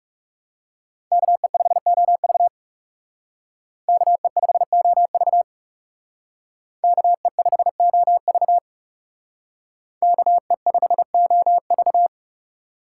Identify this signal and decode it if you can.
Morse code, heard as a keyed tone, about 38 wpm, 700 Hz. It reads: KE5OV KE5OV KE5OV KE5OV